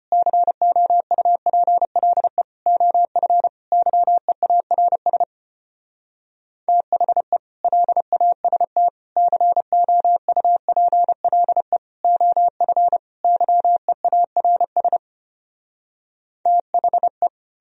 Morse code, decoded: COUPLE OF YEARS THE LAST COUPLE OF YEARS THE

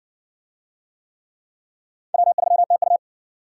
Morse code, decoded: W3TU

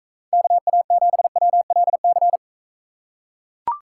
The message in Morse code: KA7WLC

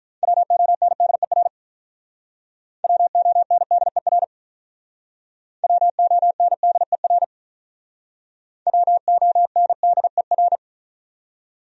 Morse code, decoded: WONDER WONDER WONDER WONDER

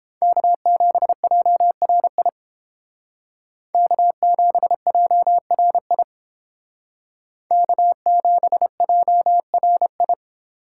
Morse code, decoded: K7JRI K7JRI K7JRI

